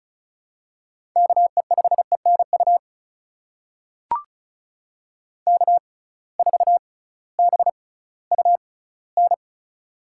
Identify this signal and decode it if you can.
Morse code, heard as a keyed tone, about 36 wpm, spaced as for 19 wpm, 700 Hz. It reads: KE5ENU K4BUN
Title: USA Call Signs - 6x Character Spacing - 35wpm